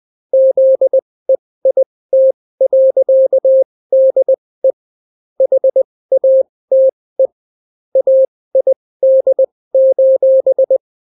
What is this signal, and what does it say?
Morse code, heard as a keyed tone, about 20 wpm, spaced as for 16 wpm, 530 Hz. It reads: ZEIT.DE HATE AID: